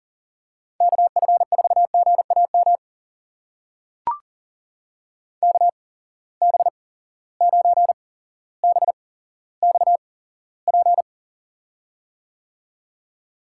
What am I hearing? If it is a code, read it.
Morse code, heard as a keyed tone, about 40 wpm, spaced as for 17 wpm, 700 Hz. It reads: KF4GAM KB9BXP
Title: USA Call Signs - 6 Characters - 8x Character Spacing - 40wpm